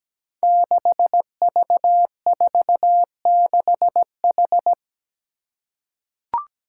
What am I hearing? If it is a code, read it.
Morse code, heard as a keyed tone, about 17 wpm, 700 Hz. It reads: 6V46H
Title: Five Challenging Letter-Number Combinations 17wpm